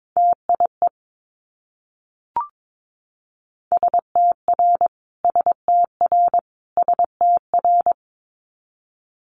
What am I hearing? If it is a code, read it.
Morse code, heard as a keyed tone, about 22 wpm, 700 Hz. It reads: TIE STR STR STR
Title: Multi-letter Phonemes Repeated 3x 22wpm